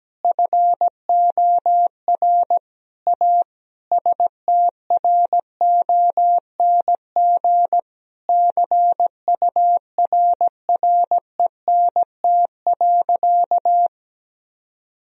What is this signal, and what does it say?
Morse code, heard as a keyed tone, about 17 wpm, 700 Hz. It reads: FOR A STRONG CURRENT.
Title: Sentences from Top 1000 Words - Review Part 4 - 17wpm